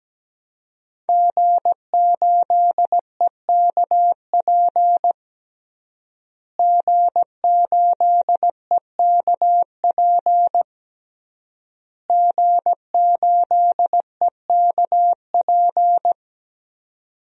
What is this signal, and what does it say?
Morse code, heard as a keyed tone, about 17 wpm, 700 Hz. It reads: G8EKP G8EKP G8EKP
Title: DXCC-100 Call Signs - 5 Characters - Repeated 3x - 17wpm